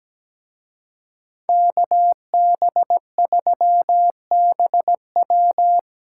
Morse code, heard as a keyed tone, about 20 wpm, 700 Hz. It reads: KB3BW
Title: Speed Racing - USA Call Signs 17wpm